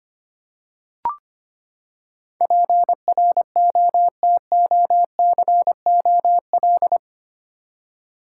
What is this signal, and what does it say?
Morse code, heard as a keyed tone, about 25 wpm, 700 Hz. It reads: PROTOCOL
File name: Eight Letter Words - 25wpm